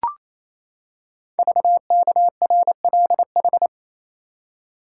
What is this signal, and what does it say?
Morse code, heard as a keyed tone, about 28 wpm, 700 Hz. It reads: VKRLH
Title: Code Groups - 5 Letters - 28wpm